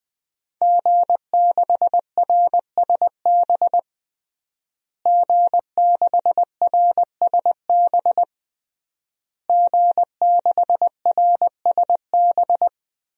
Morse code, heard as a keyed tone, about 21 wpm, 700 Hz. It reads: G6RSB G6RSB G6RSB
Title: DXCC-100 Call Signs - 5 Characters - Repeated 3x - 20wpm